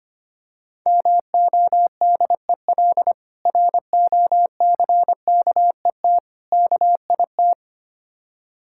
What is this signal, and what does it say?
Morse code, heard as a keyed tone, about 25 wpm, 690 Hz. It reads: MODEL ROCKET KIT
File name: practice / 25wpm / Difficult Words with Hints - 25wpm